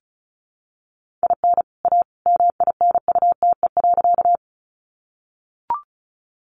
Morse code, heard as a keyed tone, about 34 wpm, 700 Hz. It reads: IN A MINUTE.
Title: Sentences from Top 500 Words 35wpm